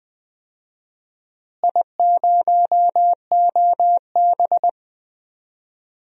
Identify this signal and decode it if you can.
Morse code, heard as a keyed tone, about 20 wpm, 700 Hz. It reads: I0OB
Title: DXCC-100 Call Signs - 4 Characters - Repeated 3x - 20wpm